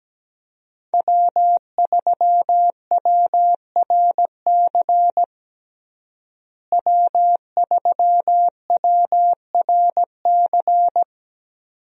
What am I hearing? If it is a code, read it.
Morse code, heard as a keyed tone, about 17 wpm, 700 Hz. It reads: W3WRC W3WRC
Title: USA Call Signs - 5 Characters - Repeated 2x - 17wpm